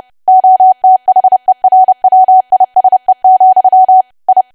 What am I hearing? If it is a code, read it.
Morse code, heard as a keyed tone, about 30 wpm, 750 Hz. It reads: OTHERWISE, I
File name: cwa / W1AW-Level3 / 30WPM